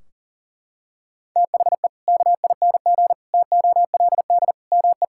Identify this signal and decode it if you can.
Morse code, heard as a keyed tone, about 40 wpm, 700 Hz. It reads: THE KING TOLD ME
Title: Sentences from Top 1600 Words 40wpm